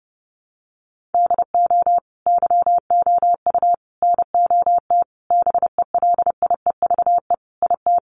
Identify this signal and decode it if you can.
Morse code, heard as a keyed tone, about 30 wpm, 700 Hz. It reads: DO YOU NOT BELIEVE IT
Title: Sentences from Top 1000 Words 30wpm